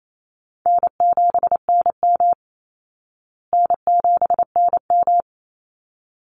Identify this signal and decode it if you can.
Morse code, heard as a keyed tone, about 28 wpm, 700 Hz. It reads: N7NM N7NM
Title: Easy Call Signs Repeated 3x - 28wpm